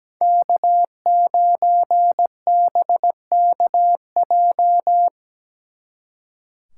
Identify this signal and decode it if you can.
Morse code, heard as a keyed tone, about 20 wpm, 700 Hz. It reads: K9BKJ
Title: Speed Racing - USA 1x3 Call Signs 17wpm